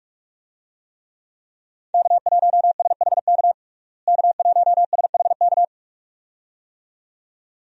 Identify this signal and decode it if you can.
Morse code, heard as a keyed tone, about 46 wpm, 700 Hz. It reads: K1SHK K1SHK
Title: USA Call Signs - 5 Characters - Repeated 2x - 45wpm